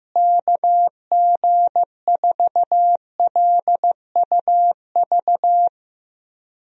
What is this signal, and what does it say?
Morse code, heard as a keyed tone, about 16 wpm, 700 Hz. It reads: KG4LUV
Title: USA Call Signs - 6 Characters - Repeated 3x - 15wpm